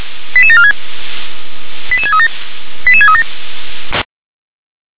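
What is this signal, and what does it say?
Five-tone selective call (ZVEI), heard as quick runs of five tones, tones of about 70 ms. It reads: 88746 88637 88637
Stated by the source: ZVEI five-tone selective call received with an SDR.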